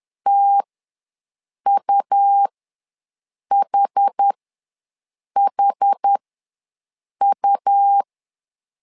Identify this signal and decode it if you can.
Morse code, heard as a keyed tone, about 10 wpm, spaced as for 6 wpm, 800 Hz. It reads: TUHHU
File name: LEZIONI / 2 UHDN / 2 UH ws 10wpm 2029